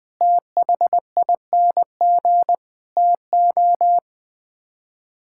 Morse code, heard as a keyed tone, about 21 wpm, 700 Hz. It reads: THING TO
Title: Build A Phrase - 5-Word Phrases - 20wpm